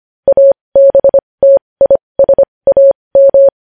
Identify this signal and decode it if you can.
Morse code, heard as a keyed tone, about 25 wpm, spaced as for 20 wpm, 550 Hz. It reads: ABTISAM
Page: news headlines in morse code at 20 wpm